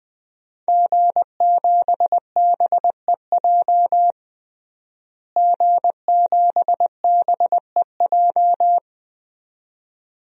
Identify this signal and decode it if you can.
Morse code, heard as a keyed tone, about 20 wpm, 700 Hz. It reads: G7BEJ G7BEJ
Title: DXCC-100 Call Signs - 5 Characters - Repeated 2x - 20wpm